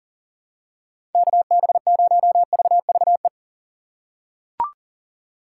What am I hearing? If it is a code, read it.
Morse code, heard as a keyed tone, about 40 wpm, 700 Hz. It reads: KB0VVE
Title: USA Call Signs - 6 Characters - 2x Character Spacing - 40wpm